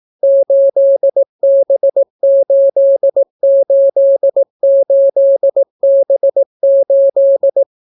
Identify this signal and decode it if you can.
Morse code, heard as a keyed tone, about 18 wpm, 550 Hz. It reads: 8B888B8